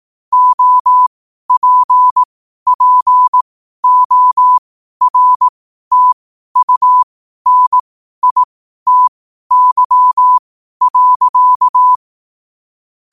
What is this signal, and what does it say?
Morse code, heard as a keyed tone, about 18 wpm, spaced as for 12 wpm, 1000 Hz. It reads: OPPORTUNITY.
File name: qotdfiles / 12wpm